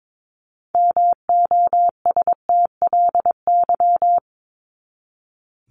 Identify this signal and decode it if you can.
Morse code, heard as a keyed tone, about 22 wpm, 700 Hz. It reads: MOSTLY
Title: Top 1001-1250 Words 22wpm